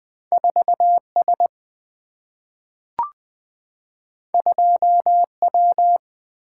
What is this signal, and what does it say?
Morse code, heard as a keyed tone, about 20 wpm, 700 Hz. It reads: 4S 2W
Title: Single-Letter Single-Number 20wpm